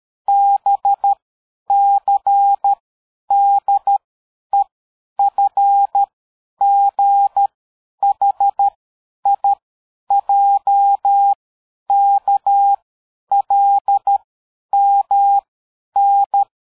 Morse code, heard as a keyed tone, about 13 wpm, 800 Hz. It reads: B C D E F G H I J K L M N
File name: dk5ke / AbisZ 60BpM